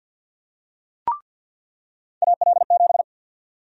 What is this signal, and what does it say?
Morse code, heard as a keyed tone, about 46 wpm, 700 Hz. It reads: AL7
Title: USA Call Sign Prefixes - 50wpm